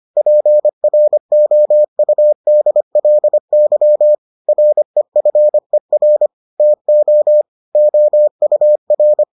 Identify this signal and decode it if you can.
Morse code, heard as a keyed tone, about 25 wpm, 600 Hz. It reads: PROUDLY REFER TO OUR